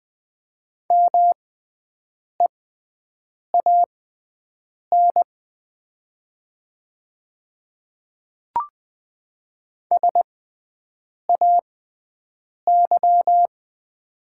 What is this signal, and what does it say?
Morse code, heard as a keyed tone, about 20 wpm, spaced as for 7 wpm, 700 Hz. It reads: MEAN SAY
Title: 6x Character Spacing - Top 300 Words - 20wpm